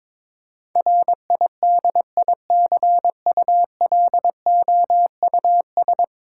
Morse code, heard as a keyed tone, about 22 wpm, 700 Hz. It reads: RIDICULOUS
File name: practice / 22wpm / 10 Letter Words - First 3 Letters Given - 22wpm